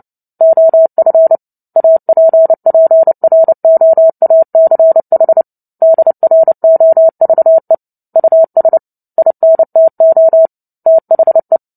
Morse code, heard as a keyed tone, about 30 wpm, 650 Hz. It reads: OF APPROACH DROVE US INTO THE
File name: Book Two / 03-Chapter 3 - 29 WPM